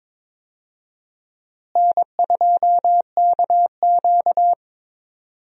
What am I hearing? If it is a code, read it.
Morse code, heard as a keyed tone, about 22 wpm, 700 Hz. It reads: N2KQ
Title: USA Call Signs - 4 Characters - Repeated 2x - 22wpm